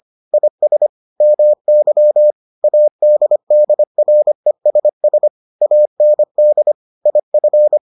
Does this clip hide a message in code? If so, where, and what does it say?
Yes: Morse code from 0.3 to 7.8 s, heard as a keyed tone, about 25 wpm, 600 Hz. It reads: IS MY ADDRESS AND IF